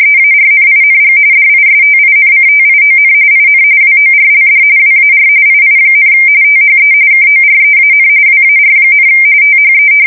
RTTY, about 45.45 baud. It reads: EARNING DISTINCTION IN WORLD WAR II, THE KOREAN WAR, THE
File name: fsk170 / battleshipnj